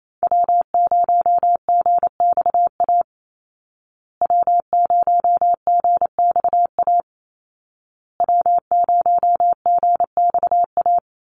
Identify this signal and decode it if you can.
Morse code, heard as a keyed tone, about 28 wpm, 700 Hz. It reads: W0GXA W0GXA W0GXA
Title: Call Signs Repeated 2x - 28wpm